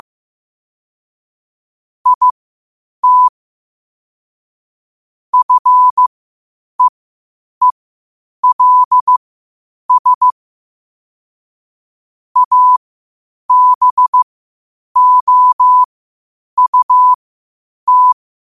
Morse code, heard as a keyed tone, about 15 wpm, spaced as for 8 wpm, 1000 Hz. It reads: IT FEELS ABOUT
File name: qotdfiles / 8wpm